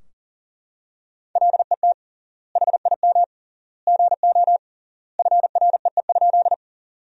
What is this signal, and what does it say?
Morse code, heard as a keyed tone, about 40 wpm, 700 Hz. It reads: LET HIM GO FREE?
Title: 3x Word-Spacing - Sentences from Top 500 Words - 40wpm